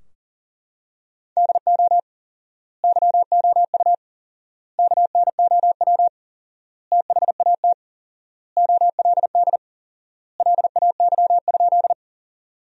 Morse code, heard as a keyed tone, about 40 wpm, 700 Hz. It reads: DO YOU KNOW THAT OLD LAY?
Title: 4x Word-Spacing - Sentences from Top 400 Words - 40wpm